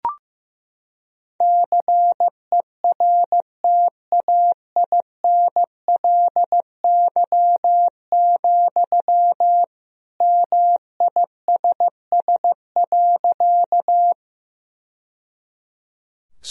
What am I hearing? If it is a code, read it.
Morse code, heard as a keyed tone, about 15 wpm, 700 Hz. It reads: CERTAINLY, MISS.
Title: Sentences from Top 1000 Words - Review Part 3 - 15wpm